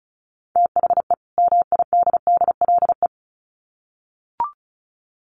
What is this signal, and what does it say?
Morse code, heard as a keyed tone, about 34 wpm, 700 Hz. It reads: THE MIDDLE
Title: Sets of 2 Words 35wpm